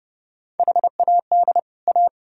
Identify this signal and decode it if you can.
Morse code, heard as a keyed tone, about 30 wpm, 700 Hz. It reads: HAD A